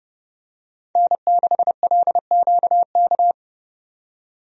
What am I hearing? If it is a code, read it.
Morse code, heard as a keyed tone, about 30 wpm, 700 Hz. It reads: N6LQK
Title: USA Call Signs - 5 Characters - 30wpm